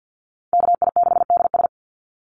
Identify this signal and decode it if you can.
Morse code, heard as a keyed tone, about 50 wpm, 700 Hz. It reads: KI6DS
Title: NA SOTA Chaser Callsigns 50wpm